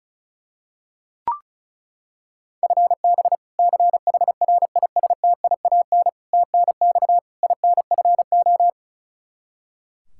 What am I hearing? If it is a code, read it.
Morse code, heard as a keyed tone, about 36 wpm, 700 Hz. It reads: FB CHRISTIAN TNX INFO
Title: K1USN SST QSO Practice 35wpm